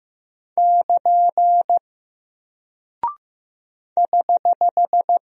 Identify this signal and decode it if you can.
Morse code, heard as a keyed tone, about 15 wpm, 700 Hz. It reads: <KN> <HH>
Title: Single Character (Letter:Number:Punctuation:Prosign) Alt 15wpm